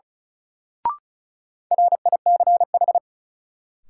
Morse code, heard as a keyed tone, about 36 wpm, 700 Hz. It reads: RICH
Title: Sentences from Top 1100 Words - New Word Drill Alt - 35wpm